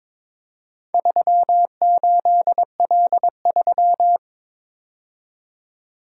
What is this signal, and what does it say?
Morse code, heard as a keyed tone, about 23 wpm, 700 Hz. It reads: 38L3
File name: practice / 22wpm / Four Letter-Number Combinations - 22wpm